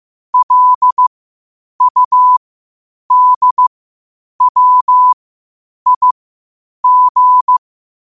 Morse code, heard as a keyed tone, about 15 wpm, spaced as for 8 wpm, 1000 Hz. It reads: LUDWIG
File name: qotdfiles / 8wpm